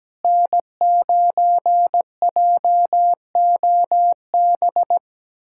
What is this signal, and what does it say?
Morse code, heard as a keyed tone, about 17 wpm, 700 Hz. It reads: N9JOB
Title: USA Callsign DE Callsign - 5 Characters - 17wpm